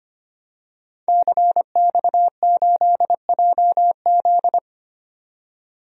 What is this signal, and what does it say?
Morse code, heard as a keyed tone, about 25 wpm, 700 Hz. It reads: CX8JZ
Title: DXCC-100 Call Signs - 5 Characters - 25wpm